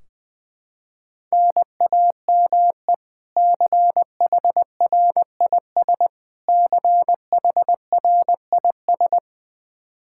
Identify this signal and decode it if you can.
Morse code, heard as a keyed tone, about 20 wpm, 700 Hz. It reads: NAME CHRIS CHRIS
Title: Practice QSO - Exchange 1 - Beginning - 20wpm